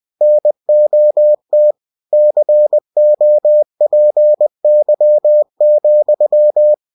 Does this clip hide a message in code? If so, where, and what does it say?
Yes: Morse code from 0.2 to 6.7 s, heard as a keyed tone, about 20 wpm, 600 Hz. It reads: NOT COPY,